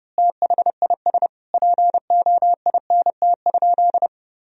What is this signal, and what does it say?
Morse code, heard as a keyed tone, about 30 wpm, 700 Hz. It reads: THIS POINT?
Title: Sentences from Top 1000 Words - Review Part 1 - 30wpm